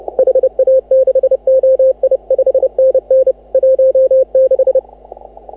Morse code, heard as a keyed tone, about 32 wpm, 530 Hz. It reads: HA6OI 5NN 16